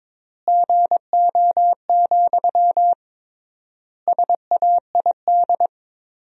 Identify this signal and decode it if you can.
Morse code, heard as a keyed tone, about 21 wpm, 700 Hz. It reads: GO, SAID
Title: 3x Word-Spacing - Sentences from Top 400 Words - 22wpm